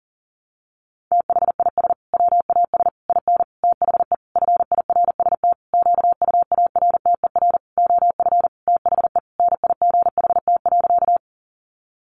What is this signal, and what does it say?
Morse code, heard as a keyed tone, about 40 wpm, 700 Hz. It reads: THIS WAS IN THE FIRST QUARTER OF THE NIGHT.
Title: Sentences from Top 1300 Words 40wpm